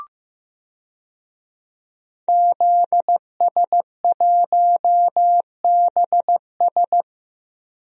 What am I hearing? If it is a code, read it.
Morse code, heard as a keyed tone, about 15 wpm, 700 Hz. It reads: ZS1BS
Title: DXCC-100 Call Signs - 5 Characters - 15wpm